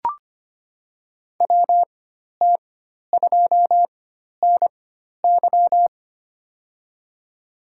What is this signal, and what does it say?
Morse code, heard as a keyed tone, about 25 wpm, spaced as for 12 wpm, 700 Hz. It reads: WT2NY
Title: USA Call Signs - 4x Character Spacing - 25wpm